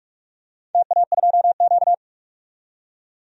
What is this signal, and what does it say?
Morse code, heard as a keyed tone, about 42 wpm, 700 Hz. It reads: TA2Q
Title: DXCC-100 Call Signs - 4 Characters - 45wpm